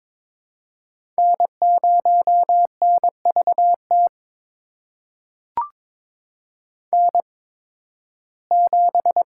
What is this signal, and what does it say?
Morse code, heard as a keyed tone, about 22 wpm, spaced as for 14 wpm, 700 Hz. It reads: N0NVT N7
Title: USA Call Signs - 5 Characters - 8x Character Spacing - 22wpm